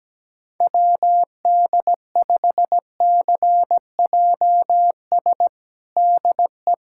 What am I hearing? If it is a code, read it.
Morse code, heard as a keyed tone, about 17 wpm, 700 Hz. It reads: WD5CJS DE
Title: USA Callsign DE Callsign 17wpm